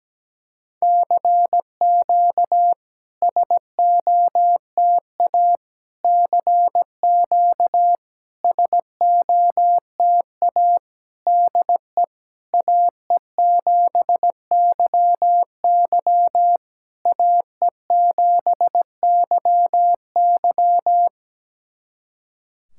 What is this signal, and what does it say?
Morse code, heard as a keyed tone, about 17 wpm, 700 Hz. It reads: CQ SOTA CQ SOTA DE AE7YY AE7YY
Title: SOTA QSO Practice 17wpm